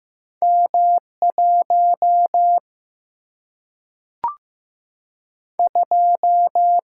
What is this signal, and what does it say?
Morse code, heard as a keyed tone, about 15 wpm, 700 Hz. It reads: M 1 2
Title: Single-Letter Single-Number 15wpm